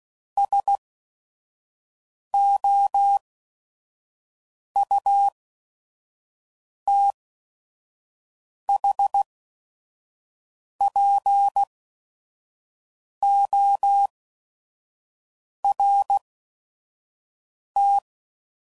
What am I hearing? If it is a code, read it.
Morse code, heard as a keyed tone, about 16 wpm, spaced as for 5 wpm, 800 Hz. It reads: SOUTHPORT
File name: CW Code Groups @5WPM / fromSM's / Cities